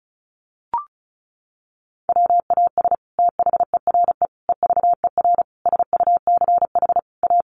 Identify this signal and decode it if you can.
Morse code, heard as a keyed tone, about 34 wpm, 700 Hz. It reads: WAS THERE EVER SUCH A
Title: Sentences from Top 400 Words 35wpm